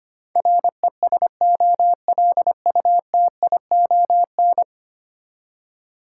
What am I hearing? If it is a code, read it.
Morse code, heard as a keyed tone, about 25 wpm, 700 Hz. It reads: RESOLUTION
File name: 10 Letter Words - First 3 Letters Given - 25wpm